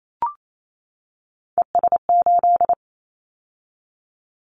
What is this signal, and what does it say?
Morse code, heard as a keyed tone, about 28 wpm, 700 Hz. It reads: ES8
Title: Callsign Prefixes for Top 100 DXCC - 28wpm